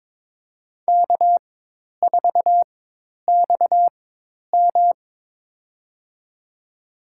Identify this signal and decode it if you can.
Morse code, heard as a keyed tone, about 22 wpm, spaced as for 10 wpm, 700 Hz. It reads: K4XM
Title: USA Call Signs - 4x Character Spacing - 22wpm